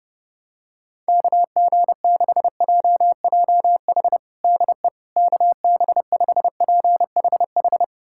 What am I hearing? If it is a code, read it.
Morse code, heard as a keyed tone, about 30 wpm, 700 Hz. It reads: KG6JJH DE KB5PHH
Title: USA Callsign DE Callsign 30wpm